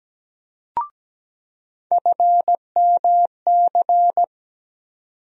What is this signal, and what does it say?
Morse code, heard as a keyed tone, about 17 wpm, 700 Hz. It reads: FMC
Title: USA Call Sign Postfixes - 3 Letters 17wpm